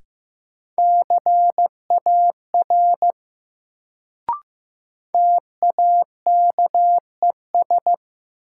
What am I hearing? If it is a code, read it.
Morse code, heard as a keyed tone, about 15 wpm, 700 Hz. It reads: CAR TAKES
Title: Sets of 5 Words Encore - New Word Drill - 15wpm